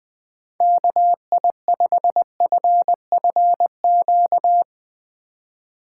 Morse code, heard as a keyed tone, about 20 wpm, 700 Hz. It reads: KI5FFQ
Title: Speed Racing - USA 2x3 Call Signs 17wpm